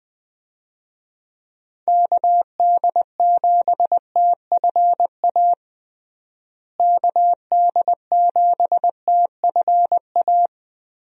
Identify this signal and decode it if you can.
Morse code, heard as a keyed tone, about 21 wpm, 700 Hz. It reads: KD7TFA KD7TFA
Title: USA Call Signs - 6 Characters - Repeated 3x - 20wpm